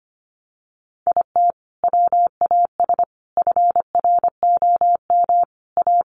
Morse code, heard as a keyed tone, about 25 wpm, 690 Hz. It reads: IT WAS FROM A
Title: Sentences from Top 1100 Words 25wpm